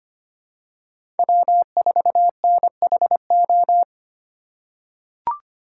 Morse code, heard as a keyed tone, about 25 wpm, 700 Hz. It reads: W4NHO
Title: USA Call Signs - 5 Characters - 2x Character Spacing - 25wpm